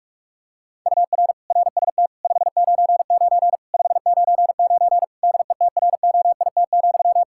Morse code, heard as a keyed tone, about 41 wpm, 700 Hz. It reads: UR RST 599 599 DETROIT,
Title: Practice QSO - Exchange 1 - Beginning - 45wpm